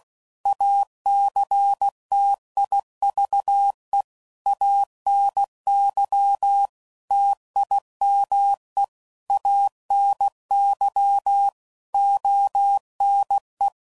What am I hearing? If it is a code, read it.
Morse code, heard as a keyed tone, about 16 wpm, 800 Hz. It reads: ACTIVE ANY TIME ANY ONE
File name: CW Code Groups @15WPM / Technical Subjects / OR Gate